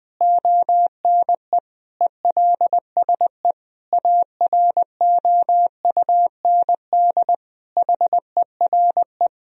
Morse code, heard as a keyed tone, about 21 wpm, 700 Hz. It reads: ONE ELSE AROUND HERE
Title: Sentences from Top 1000 Words - Review Part 3 - 20wpm